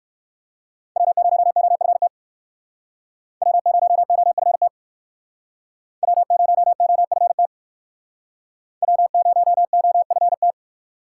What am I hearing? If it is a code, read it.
Morse code, heard as a keyed tone, about 59 wpm, 690 Hz. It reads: W0OFT W0OFT W0OFT W0OFT